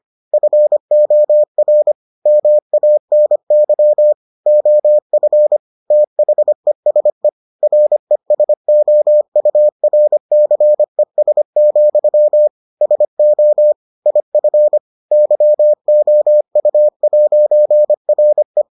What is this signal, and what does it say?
Morse code, heard as a keyed tone, about 25 wpm, 600 Hz. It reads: FOR MANY OF THESE RESOURCES, SO IF YOU'RE